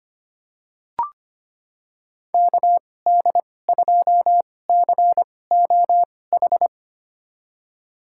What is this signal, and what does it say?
Morse code, heard as a keyed tone, about 25 wpm, spaced as for 18 wpm, 700 Hz. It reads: KD2COH
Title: USA Call Signs - 6 Characters - 2x Character Spacing - 25wpm